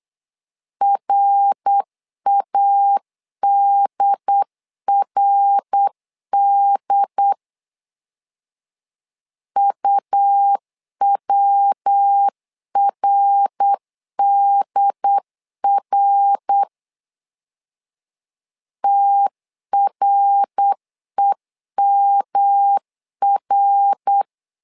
Morse code, heard as a keyed tone, about 8 wpm, 800 Hz. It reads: RADRD UWRDR TREMR
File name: LEZIONI / 3 MWRO / 3 MWR 8wpm 2444